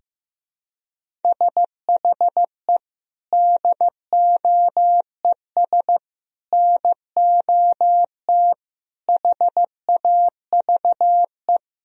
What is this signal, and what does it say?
Morse code, heard as a keyed tone, about 15 wpm, 700 Hz. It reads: SHE DOES NOT HAVE
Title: Sentences from Top 1000 Words - Review Part 3 - 15wpm